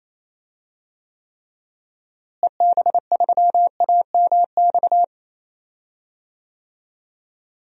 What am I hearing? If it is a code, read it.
Morse code, heard as a keyed tone, about 28 wpm, 700 Hz. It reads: EB3AMX
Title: DXCC-100 Call Signs - 6 Characters - Repeated 3x - 28wpm